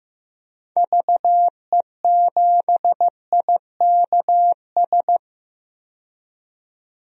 Morse code, heard as a keyed tone, about 15 wpm, 700 Hz. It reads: VE7IKS
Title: DXCC-100 Call Signs - 6 Characters - 15wpm